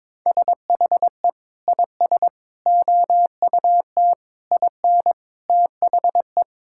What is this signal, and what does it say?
Morse code, heard as a keyed tone, about 22 wpm, 700 Hz. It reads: SHE IS OUT IN THE
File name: practice / 22wpm / Sentences with 3-Word Phrases - 22wpm